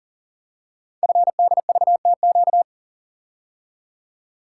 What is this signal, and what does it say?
Morse code, heard as a keyed tone, about 40 wpm, 700 Hz. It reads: FDVTQ
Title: Code Groups - 5 Letters - 40wpm